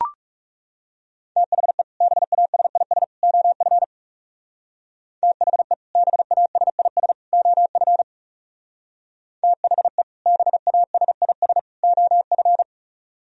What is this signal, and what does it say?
Morse code, heard as a keyed tone, about 34 wpm, 700 Hz. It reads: THE BASIS OF THE BASIS OF THE BASIS OF
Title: Speed-Racing - Sets of 3 Words 30wpm